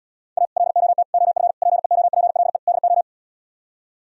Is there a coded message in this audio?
Yes: Morse code from 0.4 to 3.0 s, heard as a keyed tone, about 77 wpm, 700 Hz. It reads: A LOT OF PEOPLE DO